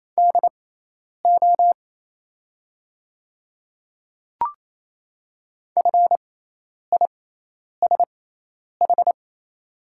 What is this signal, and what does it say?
Morse code, heard as a keyed tone, about 28 wpm, spaced as for 10 wpm, 690 Hz. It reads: DO FISH
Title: 6x Character Spacing - Top 500 Words - 28wpm